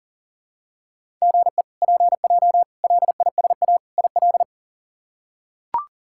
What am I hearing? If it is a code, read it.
Morse code, heard as a keyed tone, about 40 wpm, 700 Hz. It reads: GE PJ LISA IL
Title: K1USN SST QSO Practice - 40wpm